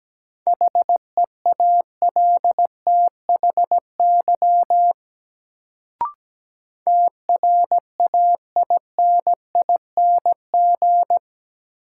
Morse code, heard as a keyed tone, about 17 wpm, 700 Hz. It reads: HEALTHY TRAINING
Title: Sentences from Top 1400 Words - New Word Drill Alt - 17wpm